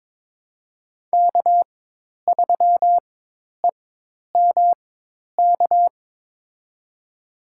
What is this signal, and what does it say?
Morse code, heard as a keyed tone, about 22 wpm, spaced as for 10 wpm, 700 Hz. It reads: K3EMK
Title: USA Call Signs - 5 Characters - 4x Character Spacing - 22wpm